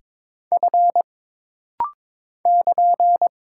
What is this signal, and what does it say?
Morse code, heard as a keyed tone, about 22 wpm, 700 Hz. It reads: F <KN>
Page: This rapid-fire timing is geared to push your ICR (Instant Character Recognition) skills.